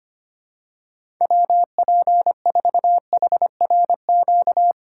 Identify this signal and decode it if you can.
Morse code, heard as a keyed tone, about 25 wpm, 700 Hz. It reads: WP4HRQ
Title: USA Call Signs - 6 Characters - 2x Character Spacing - 25wpm